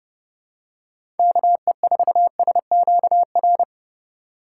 Morse code, heard as a keyed tone, about 30 wpm, 700 Hz. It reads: KE4SQR
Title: USA Call Signs - 6 Characters - 3x Character Spacing - 30wpm